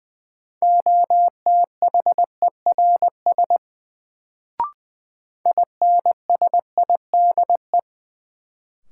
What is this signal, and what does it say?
Morse code, heard as a keyed tone, about 20 wpm, 700 Hz. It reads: OTHERS INSIDE
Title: Sentences from Top 1000 Words - Review Part 2 - New Word Drill - 20wpm